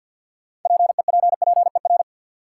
Morse code, heard as a keyed tone, about 52 wpm, 700 Hz. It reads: PEPPER
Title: Sets of 5 Words - New Word Drill - 50wpm